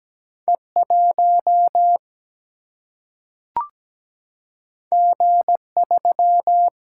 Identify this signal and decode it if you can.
Morse code, heard as a keyed tone, about 17 wpm, 700 Hz. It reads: E1 G3
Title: Single-Letter Single-Number 17wpm